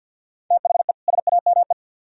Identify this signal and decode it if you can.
Morse code, heard as a keyed tone, about 63 wpm, 700 Hz. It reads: THE SAME